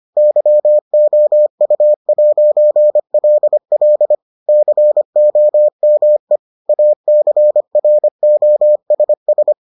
Morse code, heard as a keyed tone, about 25 wpm, 600 Hz. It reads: YOU'LL COME ACROSS